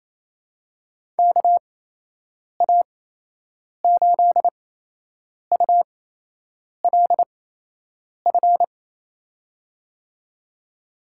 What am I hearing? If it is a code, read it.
Morse code, heard as a keyed tone, about 28 wpm, spaced as for 8 wpm, 700 Hz. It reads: KA8ULF
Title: USA Call Signs - 8x Character Spacing - 28wpm